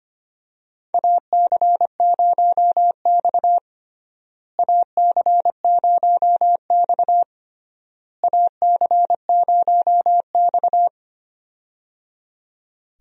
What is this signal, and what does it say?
Morse code, heard as a keyed tone, about 25 wpm, 700 Hz. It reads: AC0X AC0X AC0X
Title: USA Call Signs - 4 Characters - Repeated 3x - 25wpm